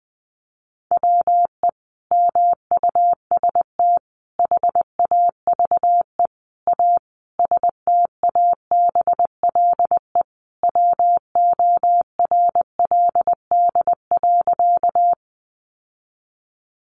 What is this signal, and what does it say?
Morse code, heard as a keyed tone, about 19 wpm, 700 Hz. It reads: WE MUST HAVE A STABLE WORLD.
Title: Sentences from Top 1500 Words 20wpm